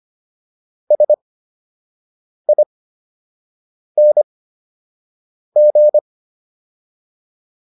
Morse code, heard as a keyed tone, about 25 wpm, spaced as for 6 wpm, 600 Hz. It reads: SING